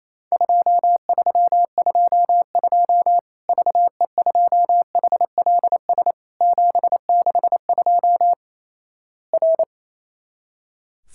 Morse code, heard as a keyed tone, about 28 wpm, 700 Hz. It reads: 2322 VE2HLS 762 R
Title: Worked All Europe DX Contest - QTC Exchange - 28wpm